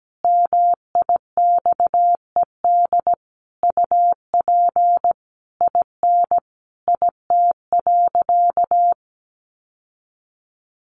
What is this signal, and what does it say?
Morse code, heard as a keyed tone, about 17 wpm, 700 Hz. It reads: MIXED UP IN IT.
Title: Sentences from Top 1500 Words 17wpm